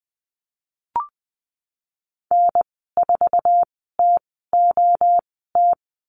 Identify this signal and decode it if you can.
Morse code, heard as a keyed tone, about 20 wpm, spaced as for 14 wpm, 700 Hz. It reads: N4TOT
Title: 055 Lesson 10g - Letter r - Call Signs 20wpm